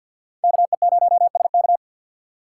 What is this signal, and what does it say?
Morse code, heard as a keyed tone, about 62 wpm, 700 Hz. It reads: KE0SK
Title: Speed Racing - USA 2x2 Call Signs 50wpm